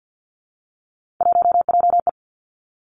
Morse code, heard as a keyed tone, about 50 wpm, 700 Hz. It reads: 1JE
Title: Three Challenging Letter-Number Combinations 50wpm